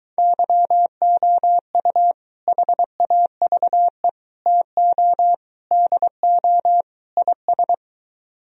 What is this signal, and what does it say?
Morse code, heard as a keyed tone, about 23 wpm, 700 Hz. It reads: YOU HAVE TO DO IS